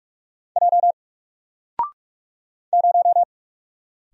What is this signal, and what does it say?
Morse code, heard as a keyed tone, about 48 wpm, 700 Hz. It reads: J 0
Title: Single Letter-Number Alt 45wpm